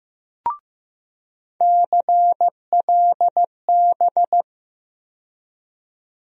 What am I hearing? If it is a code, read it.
Morse code, heard as a keyed tone, about 15 wpm, 700 Hz. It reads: CLB
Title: USA Call Sign Postfixes - 3 Letters 15wpm